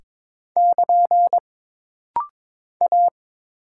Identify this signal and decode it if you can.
Morse code, heard as a keyed tone, about 22 wpm, 700 Hz. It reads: <KN> A
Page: This rapid-fire timing is geared to push your ICR (Instant Character Recognition) skills.